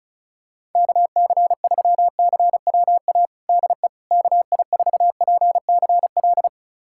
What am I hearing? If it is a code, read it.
Morse code, heard as a keyed tone, about 34 wpm, 700 Hz. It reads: KC3CWA DE KI4PCL
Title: USA Callsign DE Callsign 35wpm